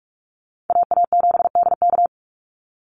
Morse code, heard as a keyed tone, about 44 wpm, 700 Hz. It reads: AA7DK
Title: NA SOTA Chaser Callsigns 45wpm